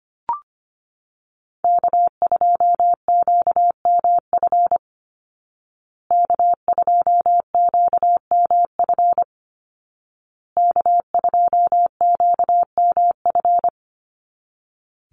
Morse code, heard as a keyed tone, about 25 wpm, 700 Hz. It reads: K2QMF K2QMF K2QMF
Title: Call Signs Repeated 2x - 25wpm